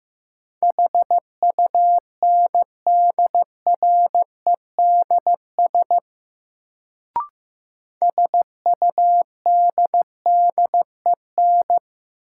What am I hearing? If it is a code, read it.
Morse code, heard as a keyed tone, about 15 wpm, 700 Hz. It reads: HUNDREDS SUDDEN
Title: Sets of 4 Words - New Word Drill - 15wpm